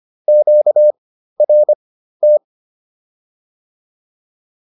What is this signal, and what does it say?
Morse code, heard as a keyed tone, about 25 wpm, spaced as for 13 wpm, 600 Hz. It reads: QRT